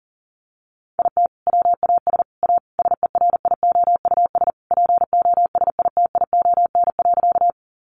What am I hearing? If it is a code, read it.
Morse code, heard as a keyed tone, about 40 wpm, 700 Hz. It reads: IT WAS A SERIOUS POSITION.
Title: Sentences from Top 1100 Words 40wpm